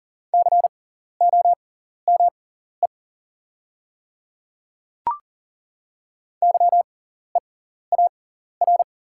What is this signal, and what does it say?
Morse code, heard as a keyed tone, about 40 wpm, spaced as for 14 wpm, 700 Hz. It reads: COME YEAR
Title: 6x Character Spacing - Top 200 Words - 40wpm